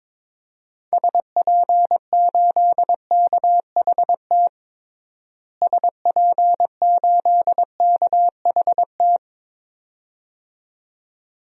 Morse code, heard as a keyed tone, about 21 wpm, 700 Hz. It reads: SP8KHT SP8KHT
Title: DXCC-100 Call Signs - 6 Characters - Repeated 2x - 22wpm